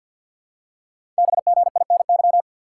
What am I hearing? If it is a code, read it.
Morse code, heard as a keyed tone, about 48 wpm, 690 Hz. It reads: BGINY
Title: Code Groups - 5 Letters - 50wpm